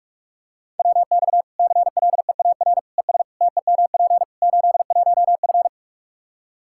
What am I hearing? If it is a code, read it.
Morse code, heard as a keyed tone, about 46 wpm, 700 Hz. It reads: WX CLEAR ES TEMP 81F
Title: Practice QSO - Exchange 2 - Info - 45wpm